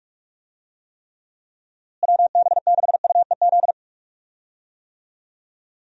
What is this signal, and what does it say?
Morse code, heard as a keyed tone, about 46 wpm, 700 Hz. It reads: WB6UEZ
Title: USA Call Signs - 6 Characters - Repeated 2x - 45wpm